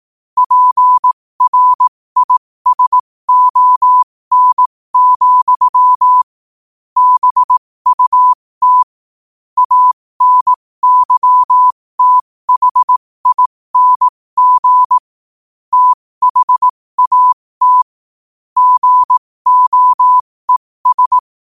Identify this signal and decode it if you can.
Morse code, heard as a keyed tone, about 18 wpm, spaced as for 15 wpm, 1000 Hz. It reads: PRISON, BUT ANYTHING THAT GOES
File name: qotdfiles / 15wpm